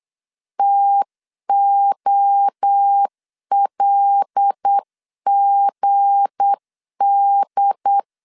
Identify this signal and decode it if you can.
Morse code, heard as a keyed tone, about 8 wpm, 800 Hz. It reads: TOLGD
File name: LEZIONI / 7 GF9Z / 7 G 8wpm 3118